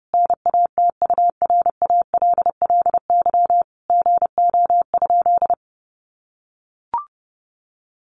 Morse code, heard as a keyed tone, about 30 wpm, 700 Hz. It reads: NATURALLY GO?
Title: Sentences from Top 1300 Words 30wpm